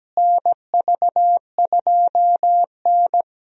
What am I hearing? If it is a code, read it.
Morse code, heard as a keyed tone, about 17 wpm, 700 Hz. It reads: NV2N
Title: USA Call Signs - 4 Characters - Repeated 3x - 17wpm